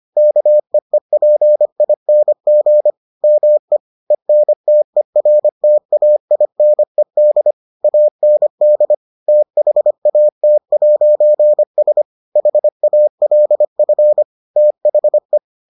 Morse code, heard as a keyed tone, about 25 wpm, 600 Hz. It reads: KEEPING ME ENTERTAINED AND THAT'S HALF THE